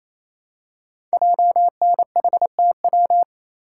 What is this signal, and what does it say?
Morse code, heard as a keyed tone, about 27 wpm, 700 Hz. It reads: JNHTW
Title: Code Groups - 5 Letters - 28wpm